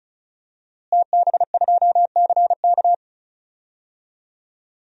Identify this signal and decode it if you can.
Morse code, heard as a keyed tone, about 36 wpm, 700 Hz. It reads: TB2CK
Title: DXCC-100 Call Signs - 5 Characters - Repeated 2x - 35wpm